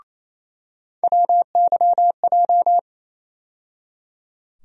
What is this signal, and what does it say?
Morse code, heard as a keyed tone, about 28 wpm, 700 Hz. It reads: WYJ
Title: USA Call Sign Postfixes - 3 Letters 28wpm